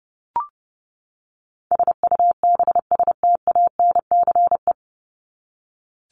Morse code, heard as a keyed tone, about 30 wpm, 700 Hz. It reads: SUBSTANCE
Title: Top 1251-1500 Words 30wpm